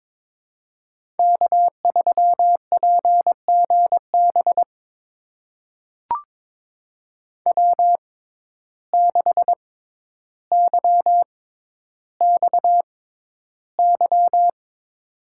Morse code, heard as a keyed tone, about 22 wpm, spaced as for 11 wpm, 700 Hz. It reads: K3PGB W6YXY
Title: USA Call Signs - 5 Characters - 6x Character Spacing - 22wpm